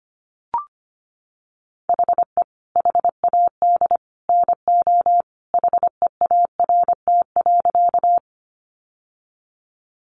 Morse code, heard as a keyed tone, about 25 wpm, 700 Hz. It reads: HE HAD NO HEART.
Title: Sentences from Top 1200 Words 25wpm